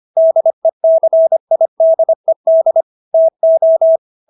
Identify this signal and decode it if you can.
Morse code, heard as a keyed tone, about 25 wpm, 650 Hz. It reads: DECIDED TO